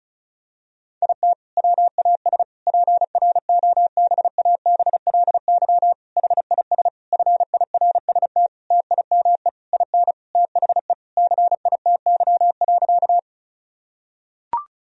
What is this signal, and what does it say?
Morse code, heard as a keyed tone, about 36 wpm, 700 Hz. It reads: IT WAS PROBABLY HIS FIRST TIME IN THE CITY.
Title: Sentences from Top 1000 Words - Review Part 4 - 35wpm